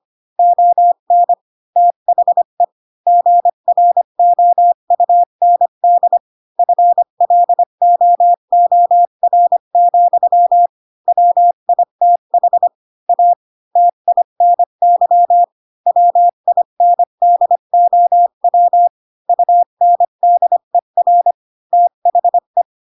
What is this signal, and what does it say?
Morse code, heard as a keyed tone, about 25 wpm, 700 Hz. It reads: ON THE GROUND FLOOR, WITH A TINY WINDOW UNDER THE